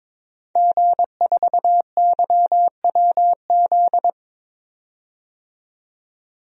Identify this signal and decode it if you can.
Morse code, heard as a keyed tone, about 22 wpm, 700 Hz. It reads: G4YWZ
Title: DXCC-100 Call Signs - 5 Characters - 22wpm